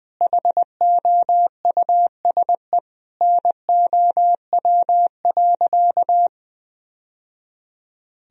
Morse code, heard as a keyed tone, about 20 wpm, 700 Hz. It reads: HOUSE NOW.
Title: Sentences from Top 1000 Words - Review Part 2 - 20wpm